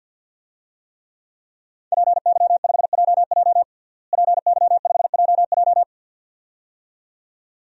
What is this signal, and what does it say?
Morse code, heard as a keyed tone, about 53 wpm, 700 Hz. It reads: PY5JJ PY5JJ
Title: DXCC-100 Call Signs - 5 Characters - Repeated 2x - 50wpm